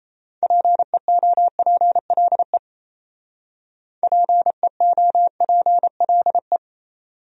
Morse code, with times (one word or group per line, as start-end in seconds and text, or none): 0.4-2.6 PEOPLE
4.0-6.6 PEOPLE